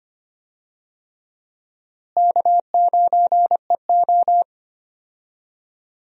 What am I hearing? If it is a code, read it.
Morse code, heard as a keyed tone, about 25 wpm, 700 Hz. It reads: K9EO
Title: USA Call Signs - 4 Characters - Repeated 3x - 25wpm